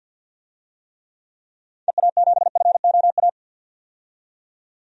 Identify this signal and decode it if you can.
Morse code, heard as a keyed tone, about 47 wpm, 700 Hz. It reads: EA7FOA
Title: DXCC-100 Call Signs - 6 Characters - 50wpm